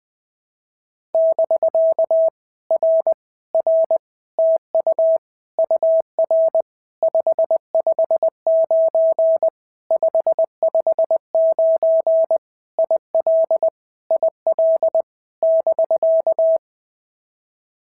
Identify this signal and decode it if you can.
Morse code, heard as a keyed tone, about 20 wpm, 650 Hz. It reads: <BK> R R TU UR 559 559 IL IL <BK>
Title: POTA QSO Practice - 20wpm